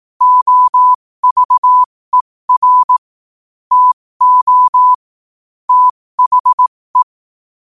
Morse code, heard as a keyed tone, about 18 wpm, spaced as for 15 wpm, 1000 Hz. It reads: OVER TO THE
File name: qotdfiles / 15wpm